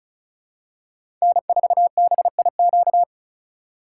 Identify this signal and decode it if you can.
Morse code, heard as a keyed tone, about 36 wpm, 700 Hz. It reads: N4BIQ
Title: USA Call Signs - 5 Characters - 2x Character Spacing - 35wpm